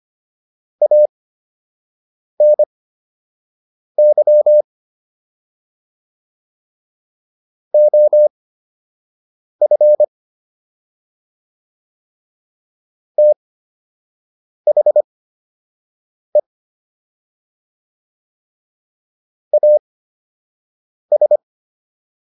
Morse code, heard as a keyed tone, about 25 wpm, spaced as for 6 wpm, 600 Hz. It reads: ANY OF THE AS